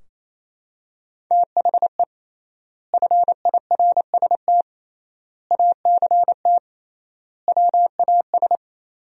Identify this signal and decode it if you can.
Morse code, heard as a keyed tone, about 28 wpm, 700 Hz. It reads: THE FIRST ACT WAS
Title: 3x Word-Spacing - Sentences from Top 200 Words - 28wpm